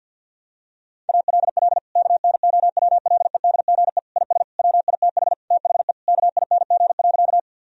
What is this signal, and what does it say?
Morse code, heard as a keyed tone, about 49 wpm, 700 Hz. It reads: ALL KNOWLEDGE IS WITH THE KING.
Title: Sentences from Top 1000 Words - Review Part 3 - 50wpm